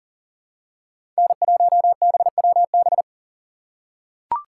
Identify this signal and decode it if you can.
Morse code, heard as a keyed tone, about 40 wpm, 700 Hz. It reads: N1BWB
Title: USA Call Signs - 5 Characters - 3x Character Spacing - 40wpm